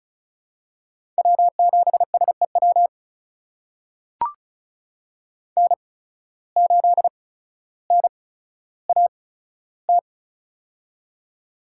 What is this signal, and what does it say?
Morse code, heard as a keyed tone, about 34 wpm, spaced as for 15 wpm, 700 Hz. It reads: W7SEW N8NAT
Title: USA Call Signs - 5 Characters - 8x Character Spacing - 35wpm